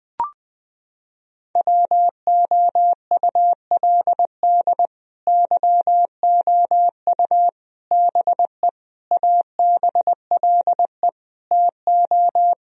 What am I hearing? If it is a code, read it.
Morse code, heard as a keyed tone, about 20 wpm, 700 Hz. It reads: WOULD YOU BE ABLE TO
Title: Sentences from Top 1000 Words - Review Part 2 - 20wpm